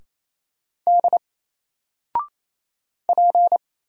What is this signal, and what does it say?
Morse code, heard as a keyed tone, about 28 wpm, 700 Hz. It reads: D P
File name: Single-Letter-Number-28wpm